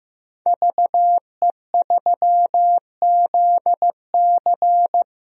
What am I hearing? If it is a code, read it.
Morse code, heard as a keyed tone, about 15 wpm, 700 Hz. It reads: VE3ZC
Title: DXCC-100 Call Signs - 5 Characters - 15wpm